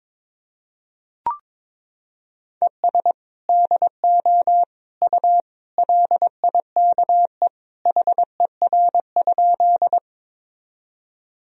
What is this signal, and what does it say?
Morse code, heard as a keyed tone, about 22 wpm, 700 Hz. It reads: ES DO U LIKE HER?
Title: Sentences from Top 100 Words and Abbreviations 22wpm